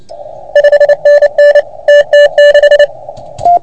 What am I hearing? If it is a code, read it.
Morse code, heard as a keyed tone, about 29 wpm, 1750 Hz. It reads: 5NN TT6
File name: Archiv / CUC381 / OK1FGD CUC 381